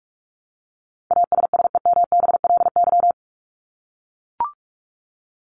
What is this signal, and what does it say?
Morse code, heard as a keyed tone, about 44 wpm, 700 Hz. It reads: ASSEMBLY
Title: LY Suffix - Top 100 Words with Base Words 45wpm